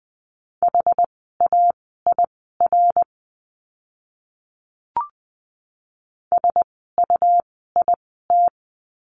Morse code, heard as a keyed tone, about 20 wpm, spaced as for 14 wpm, 700 Hz. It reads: HAIR SUIT
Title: 101 Lesson 17f - Letter u - Words 20wpm